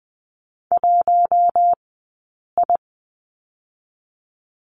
Morse code, heard as a keyed tone, about 20 wpm, 700 Hz. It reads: 1 I
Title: 086 Lesson 15d - Number 5 - Two Characters 20wpm